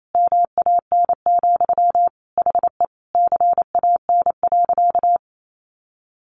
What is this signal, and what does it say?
Morse code, heard as a keyed tone, about 28 wpm, 700 Hz. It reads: MAN, HE CAN.
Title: Sentences from Top 1500 Words 28wpm